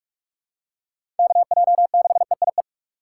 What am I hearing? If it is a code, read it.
Morse code, heard as a keyed tone, about 42 wpm, 700 Hz. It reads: KJ6EIE
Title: USA Call Signs - 6 Characters - 2x Character Spacing - 45wpm